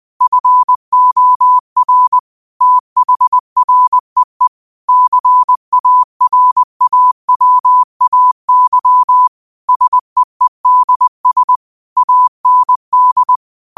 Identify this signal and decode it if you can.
Morse code, heard as a keyed tone, about 20 wpm, 1000 Hz. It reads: FOR THREE CARAWAY SEEDS AND